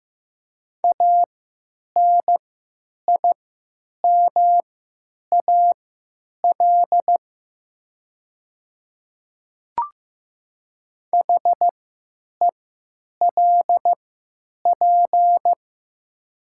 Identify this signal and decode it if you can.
Morse code, heard as a keyed tone, about 15 wpm, spaced as for 9 wpm, 700 Hz. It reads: ANIMAL HELP
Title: 3x Character Spacing - Top 200 Words - 15wpm